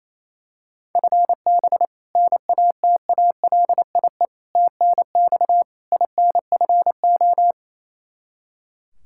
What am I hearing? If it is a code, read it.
Morse code, heard as a keyed tone, about 27 wpm, 700 Hz. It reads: FB NATALIE TNX INFO
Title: Practice QSO - Exchange 3 - More Info - 28wpm